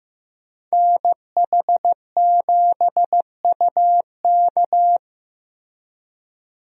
Morse code, heard as a keyed tone, about 15 wpm, 700 Hz. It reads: NH7UK
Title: USA Call Signs - 5 Characters - Repeated 2x - 15wpm